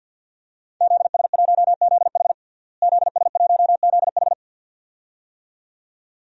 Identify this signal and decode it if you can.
Morse code, heard as a keyed tone, about 52 wpm, 700 Hz. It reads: ZS1ZH ZS1ZH
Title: DXCC-100 Call Signs - 5 Characters - Repeated 2x - 50wpm